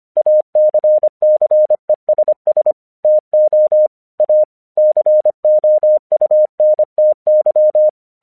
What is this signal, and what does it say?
Morse code, heard as a keyed tone, about 25 wpm, 610 Hz. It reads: ACCESS TO A COUNTY
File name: 1 English 25wpm / AQC07